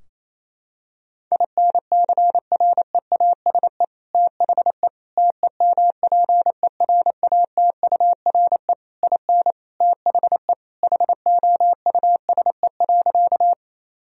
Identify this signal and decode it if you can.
Morse code, heard as a keyed tone, about 28 wpm, 700 Hz. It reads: INCREASE THE TEMPERATURE IN THE HOUSE.
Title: Sentences from Top 1000 Words - Review Part 4 - 28wpm